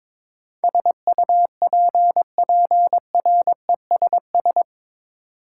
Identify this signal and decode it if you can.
Morse code, heard as a keyed tone, about 22 wpm, 700 Hz. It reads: SUPPRESS
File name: practice / 22wpm / Eight Letter Words - 22wpm